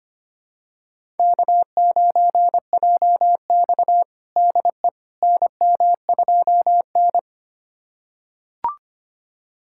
Morse code, heard as a keyed tone, about 25 wpm, 700 Hz. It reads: K9JX DE NM2N
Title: USA Callsign DE Callsign - 4 Characters - 25wpm